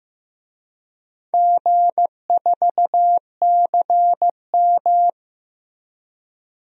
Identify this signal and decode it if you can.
Morse code, heard as a keyed tone, about 16 wpm, 700 Hz. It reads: G4CM
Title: DXCC-100 Call Signs - 4 Characters - Repeated 3x - 15wpm